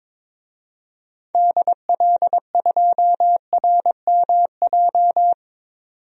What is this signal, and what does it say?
Morse code, heard as a keyed tone, about 22 wpm, 700 Hz. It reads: DL2RMJ
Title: DXCC-100 Call Signs - 6 Characters - 22wpm